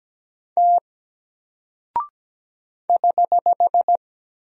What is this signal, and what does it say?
Morse code, heard as a keyed tone, about 17 wpm, 700 Hz. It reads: T <HH>
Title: Single Character (Letter:Number:Punctuation:Prosign) 17wpm